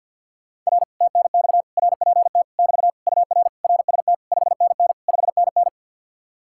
Morse code, heard as a keyed tone, about 64 wpm, 700 Hz. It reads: R TNX RPT <BT> UR RST 5NN 5NN